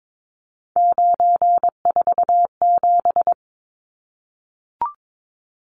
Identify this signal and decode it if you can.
Morse code, heard as a keyed tone, about 22 wpm, 700 Hz. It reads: 947
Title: Numbers - 3-Digit 22wpm